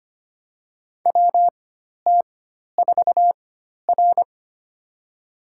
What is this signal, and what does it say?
Morse code, heard as a keyed tone, about 25 wpm, spaced as for 12 wpm, 700 Hz. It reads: WT4R
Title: USA Call Signs - 4 Characters - 4x Character Spacing - 25wpm